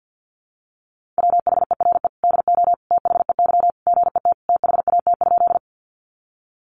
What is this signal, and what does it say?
Morse code, heard as a keyed tone, about 48 wpm, 700 Hz. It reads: WHERE DO THEY GET THAT?
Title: Sentences from Top 200 Words 50wpm